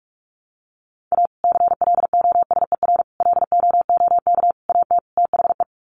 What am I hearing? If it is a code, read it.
Morse code, heard as a keyed tone, about 44 wpm, 700 Hz. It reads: A CLOSER LOOK AT THE
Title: Sets of 5 Words Encore 45wpm